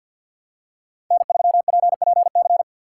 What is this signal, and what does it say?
Morse code, heard as a keyed tone, about 51 wpm, 700 Hz. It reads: N3PPC
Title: USA Call Signs - 4x Character Spacing - 50wpm